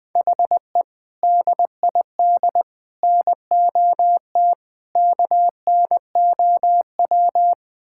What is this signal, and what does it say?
Morse code, heard as a keyed tone, about 20 wpm, 700 Hz. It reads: HE DID NOT KNOW
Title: Build A Phrase - 4-Word Phrases - 20wpm